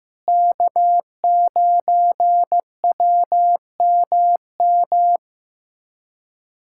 Morse code, heard as a keyed tone, about 15 wpm, 700 Hz. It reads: K9WMM
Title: USA Callsign DE Callsign - 5 Characters - 15wpm